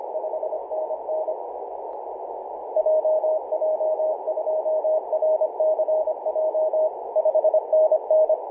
Morse code, heard as a keyed tone, about 26 wpm, 630 Hz, with noise. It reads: 5NN JJ2RCJ 5NN
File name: Apr/29 0606z VK9NT 50.115MHz CW